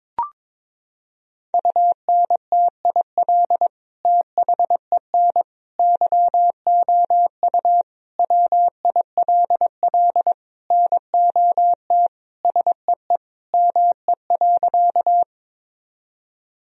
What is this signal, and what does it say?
Morse code, heard as a keyed tone, about 22 wpm, 700 Hz. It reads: UNTIL THEN YOU WILL NOT SEE ME.
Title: Sentences from Top 1000 Words - Review Part 1 - 22wpm